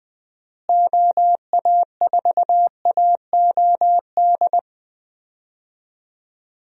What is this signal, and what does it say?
Morse code, heard as a keyed tone, about 20 wpm, 700 Hz. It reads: OA4AOD
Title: USA Call Signs - 6 Characters - Repeated 2x - 20wpm